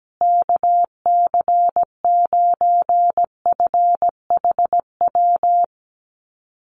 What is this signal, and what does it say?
Morse code, heard as a keyed tone, about 17 wpm, 700 Hz. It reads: KC9FHW
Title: USA Call Signs Repeated 3x 17wpm